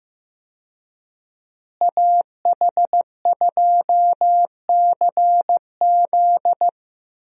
Morse code, heard as a keyed tone, about 15 wpm, 700 Hz. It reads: AH2CZ
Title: USA Call Signs - 5 Characters - Repeated 3x - 15wpm